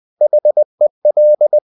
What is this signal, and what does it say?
Morse code, heard as a keyed tone, about 20 wpm, 600 Hz. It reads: HEL
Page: CW med Gnister